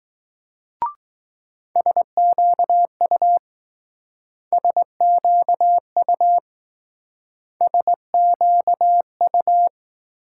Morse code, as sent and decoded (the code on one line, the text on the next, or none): ... --.- ..- / ... --.- ..- / ... --.- ..-
SQU SQU SQU